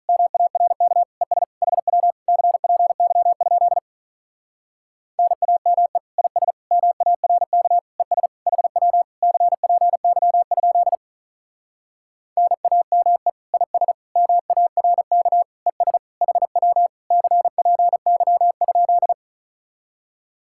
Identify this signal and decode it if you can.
Morse code, heard as a keyed tone, about 45 wpm, 700 Hz. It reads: MARK ES HW CPY? NAME IS MARK ES HW CPY? NAME IS MARK ES HW CPY?